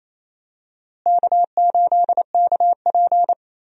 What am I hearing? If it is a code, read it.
Morse code, heard as a keyed tone, about 28 wpm, 700 Hz. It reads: K8KP
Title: USA Call Signs - 4 Characters - Repeated 3x - 28wpm